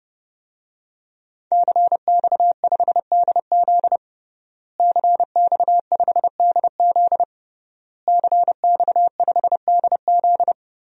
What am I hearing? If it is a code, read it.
Morse code, heard as a keyed tone, about 30 wpm, 700 Hz. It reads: CX5DZ CX5DZ CX5DZ
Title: DXCC-100 Call Signs - 5 Characters - Repeated 3x - 30wpm